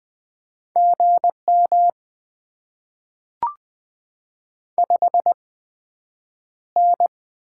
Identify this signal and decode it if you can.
Morse code, heard as a keyed tone, about 20 wpm, 700 Hz. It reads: G M 5 N
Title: Top 20 QSO Elements - 8x Character Spacing - 20wpm